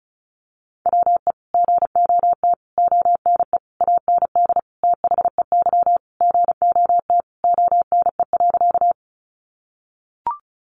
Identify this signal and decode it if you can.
Morse code, heard as a keyed tone, about 34 wpm, 700 Hz. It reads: WE GOT ONE AND THEY GOT ONE.
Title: Sentences from Top 300 Words 35wpm